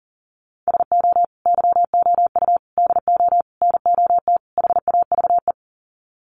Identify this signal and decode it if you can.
Morse code, heard as a keyed tone, about 40 wpm, 700 Hz. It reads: SO YOU DO NOT HAVE
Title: Sets of 5 Words Encore 40wpm